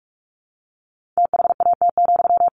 Morse code, heard as a keyed tone, about 44 wpm, 700 Hz. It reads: THAT,
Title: Sentences from Top 1500 Words 45wpm